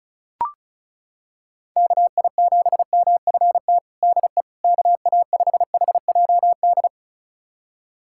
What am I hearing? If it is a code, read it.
Morse code, heard as a keyed tone, about 36 wpm, 700 Hz. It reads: KI7MFT DE KA5HJD
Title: USA Callsign DE Callsign - 6 Characters - 35wpm